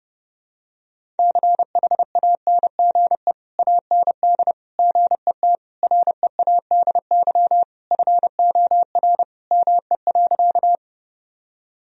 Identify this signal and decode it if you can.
Morse code, heard as a keyed tone, about 31 wpm, 700 Hz. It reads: CHANGE AND GET READY FOR ME.
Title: Sentences from Top 1000 Words - Review Part 1 - 30wpm